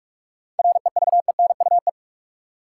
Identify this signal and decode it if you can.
Morse code, heard as a keyed tone, about 42 wpm, 700 Hz. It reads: REVENUE
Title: Seven Letter Words - 45wpm